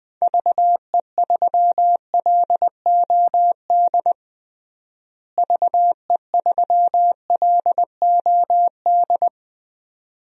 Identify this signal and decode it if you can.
Morse code, heard as a keyed tone, about 20 wpm, 700 Hz. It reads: VE3LOD VE3LOD
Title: DXCC-100 Call Signs - 6 Characters - Repeated 2x - 20wpm